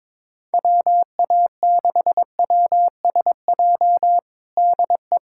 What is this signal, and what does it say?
Morse code, heard as a keyed tone, about 21 wpm, 700 Hz. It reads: WA6WSJ DE
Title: USA Callsign DE Callsign - 6 Characters - 22wpm